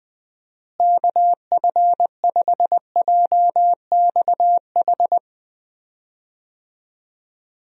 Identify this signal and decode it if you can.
Morse code, heard as a keyed tone, about 20 wpm, 700 Hz. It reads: KF5JXH
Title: USA Call Signs - 6 Characters - Repeated 2x - 20wpm